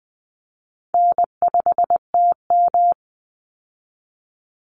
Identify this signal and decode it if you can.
Morse code, heard as a keyed tone, about 20 wpm, 700 Hz. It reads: N5TM
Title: Call Signs - 20wpm